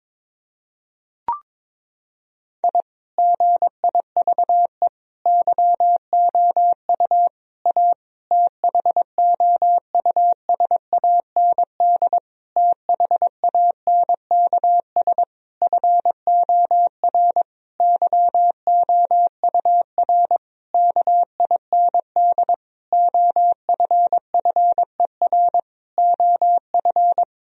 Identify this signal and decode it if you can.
Morse code, heard as a keyed tone, about 22 wpm, 700 Hz. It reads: I GIVE YOU A THOUSAND THANKS FOR YOUR KIND OFFER OF
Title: Sentences from Top 1000 Words - Review Part 4 - 22wpm